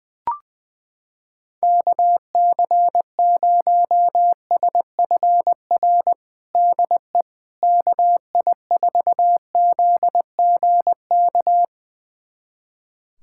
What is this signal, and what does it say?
Morse code, heard as a keyed tone, about 21 wpm, 700 Hz. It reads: KC0SFR DE KI4ZGK
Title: USA Callsign DE Callsign - 6 Characters - 20wpm